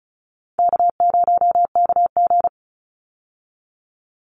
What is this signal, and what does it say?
Morse code, heard as a keyed tone, about 34 wpm, 700 Hz. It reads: K0KG
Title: Easy Call Signs Repeated 2x - 35wpm